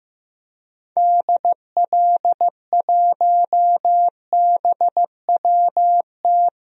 Morse code, heard as a keyed tone, about 15 wpm, 700 Hz. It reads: DL1BWT
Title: DXCC-100 Call Signs - 6 Characters - 15wpm